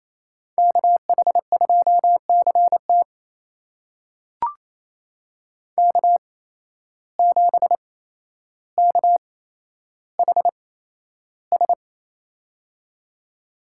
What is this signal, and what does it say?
Morse code, heard as a keyed tone, about 28 wpm, spaced as for 12 wpm, 700 Hz. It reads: KH2CT K7KHS
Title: USA Call Signs - 5 Characters - 8x Character Spacing - 28wpm